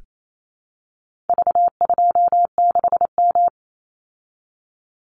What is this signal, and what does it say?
Morse code, heard as a keyed tone, about 28 wpm, 700 Hz. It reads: V26M
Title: Call Signs Repeated 2x - 28wpm